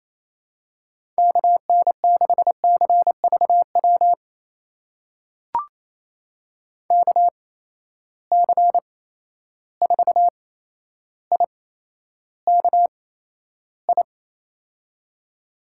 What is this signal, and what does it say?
Morse code, heard as a keyed tone, about 28 wpm, spaced as for 12 wpm, 700 Hz. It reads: KN6CVW KC4IKI
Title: USA Call Signs - 6 Characters - 8x Character Spacing - 28wpm